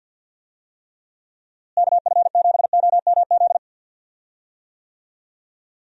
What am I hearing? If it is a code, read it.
Morse code, heard as a keyed tone, about 52 wpm, 700 Hz. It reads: KF7OMZ
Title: USA Call Signs - 6 Characters - Repeated 2x - 50wpm